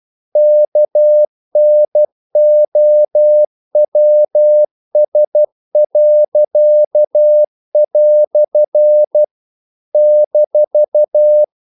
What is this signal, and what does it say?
Morse code, heard as a keyed tone, about 12 wpm, 600 Hz. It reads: KNOWS." -
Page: English quotations in CW (telegraphy - morse)